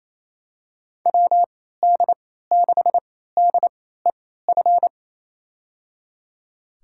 Morse code, heard as a keyed tone, about 28 wpm, spaced as for 16 wpm, 700 Hz. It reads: WD6DEF
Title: USA Call Signs - 6 Characters - 3x Character Spacing - 28wpm